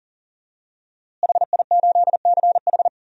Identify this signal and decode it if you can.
Morse code, heard as a keyed tone, about 40 wpm, 700 Hz. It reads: HI8CH
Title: DXCC-100 Call Signs - 5 Characters - Repeated 3x - 40wpm